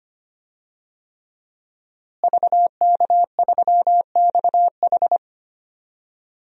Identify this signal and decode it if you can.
Morse code, heard as a keyed tone, about 25 wpm, 700 Hz. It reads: VK3XH
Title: DXCC-100 Call Signs - 5 Characters - Repeated 3x - 25wpm